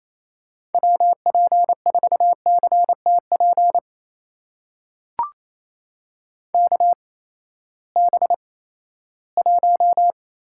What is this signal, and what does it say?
Morse code, heard as a keyed tone, about 27 wpm, spaced as for 16 wpm, 700 Hz. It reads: WP4CTP KB1
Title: USA Call Signs - 6 Characters - 8x Character Spacing - 28wpm